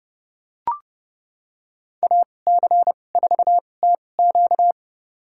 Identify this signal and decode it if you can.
Morse code, heard as a keyed tone, about 32 wpm, spaced as for 22 wpm, 700 Hz. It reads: AC4TQ
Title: USA Call Signs - 2x Character Spacing - 30wpm